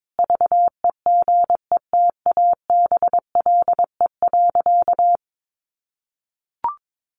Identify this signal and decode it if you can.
Morse code, heard as a keyed tone, about 22 wpm, 700 Hz. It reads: VEGETABLE.
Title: Sentences from Top 1300 Words 22wpm